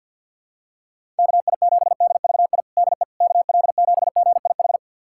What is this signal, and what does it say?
Morse code, heard as a keyed tone, about 47 wpm, 700 Hz. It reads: KI7DVI DE KL7GIH
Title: USA Callsign DE Callsign - 6 Characters - 50wpm